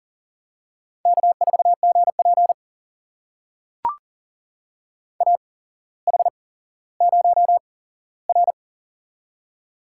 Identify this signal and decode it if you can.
Morse code, heard as a keyed tone, about 40 wpm, spaced as for 17 wpm, 700 Hz. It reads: K4GP AH0R
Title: USA Call Signs - 8x Character Spacing - 40wpm